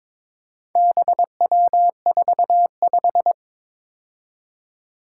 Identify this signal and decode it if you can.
Morse code, heard as a keyed tone, about 23 wpm, 700 Hz. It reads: BW45
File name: practice / 22wpm / Four Letter-Number Combinations - 22wpm